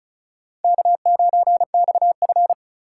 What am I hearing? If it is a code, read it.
Morse code, heard as a keyed tone, about 34 wpm, 700 Hz. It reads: K9XF
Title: USA Call Signs - 4 Characters - 4x Character Spacing - 35wpm